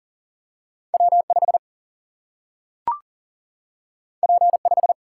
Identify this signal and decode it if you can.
Morse code, heard as a keyed tone, about 42 wpm, 700 Hz. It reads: W5 P5
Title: Single-Letter Single-Number 40wpm